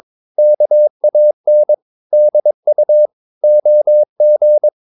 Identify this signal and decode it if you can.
Morse code, heard as a keyed tone, about 22 wpm, 600 Hz. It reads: KAN DU OG
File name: Ferie QRS 22wpm